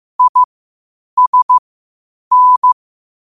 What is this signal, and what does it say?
Morse code, heard as a keyed tone, about 15 wpm, spaced as for 8 wpm, 1000 Hz. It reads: ISN
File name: qotdfiles / 8wpm